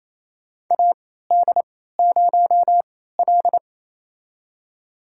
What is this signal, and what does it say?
Morse code, heard as a keyed tone, about 28 wpm, spaced as for 16 wpm, 700 Hz. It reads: AD0L
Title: USA Call Signs - 4 Characters - 3x Character Spacing - 28wpm